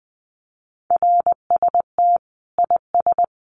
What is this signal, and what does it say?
Morse code, heard as a keyed tone, about 20 wpm, 700 Hz. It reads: RST IS
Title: Common QSO Phrases 20wpm